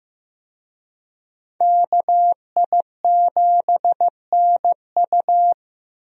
Morse code, heard as a keyed tone, about 15 wpm, 700 Hz. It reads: KI7NU
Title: USA Call Signs - 5 Characters - Repeated 3x - 15wpm